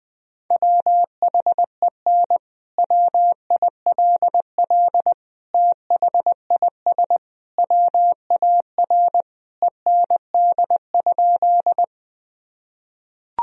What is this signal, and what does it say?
Morse code, heard as a keyed tone, about 19 wpm, 700 Hz. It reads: WHEN WILL THIS WAR END?
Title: Sentences from Top 1000 Words - Review Part 2 - 20wpm